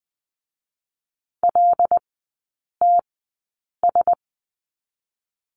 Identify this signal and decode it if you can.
Morse code, heard as a keyed tone, about 20 wpm, 700 Hz. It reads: L T S
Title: 100 Lesson 17e - Letter u - Three Characters 20wpm